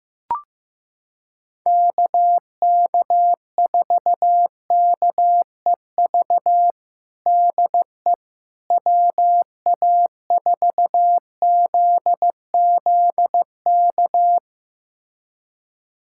Morse code, heard as a keyed tone, about 16 wpm, 700 Hz. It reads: KK4KEV DE WA4ZZK
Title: USA Callsign DE Callsign - 6 Characters - 15wpm